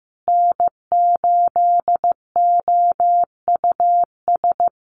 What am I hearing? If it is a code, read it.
Morse code, heard as a keyed tone, about 15 wpm, 700 Hz. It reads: N8OUS
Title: USA Call Signs Repeated 3x 15wpm